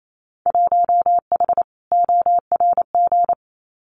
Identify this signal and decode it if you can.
Morse code, heard as a keyed tone, about 28 wpm, 700 Hz. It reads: 1H ORG
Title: Winter Field Day Exchanges 28wpm